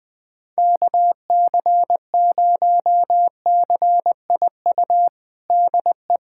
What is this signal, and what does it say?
Morse code, heard as a keyed tone, about 20 wpm, 700 Hz. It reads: KC0CIU DE
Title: USA Callsign DE Callsign - 6 Characters - 20wpm